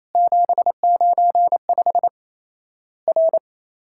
Morse code, heard as a keyed tone, about 27 wpm, 700 Hz. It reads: 795 R
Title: Worked All Europe DX Contest - QTC Exchange - 28wpm